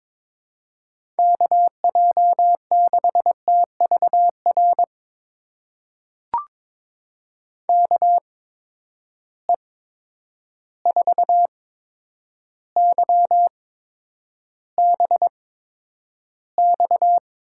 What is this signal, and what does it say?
Morse code, heard as a keyed tone, about 22 wpm, spaced as for 9 wpm, 700 Hz. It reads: KJ6TVR KE4YBX
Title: USA Call Signs - 6 Characters - 8x Character Spacing - 22wpm